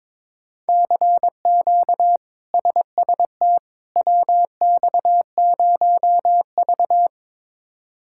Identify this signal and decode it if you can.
Morse code, heard as a keyed tone, about 22 wpm, 700 Hz. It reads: CQ SST WX0V
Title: K1USN SST QSO Practice - 22wpm